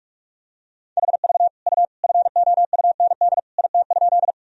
Morse code, heard as a keyed tone, about 42 wpm, 700 Hz. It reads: HV U FOUND IT?
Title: Sentences from Top 300 Words and Abbreviations 45wpm